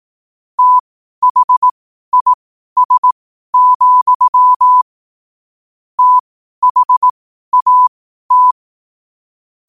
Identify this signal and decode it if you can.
Morse code, heard as a keyed tone, about 18 wpm, spaced as for 12 wpm, 1000 Hz. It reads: THIS, THAT
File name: qotdfiles / 12wpm